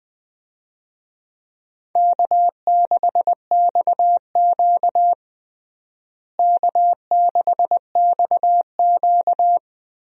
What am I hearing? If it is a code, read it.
Morse code, heard as a keyed tone, about 20 wpm, 700 Hz. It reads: K6XQ K6XQ
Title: USA Call Signs - 4 Characters - Repeated 3x - 20wpm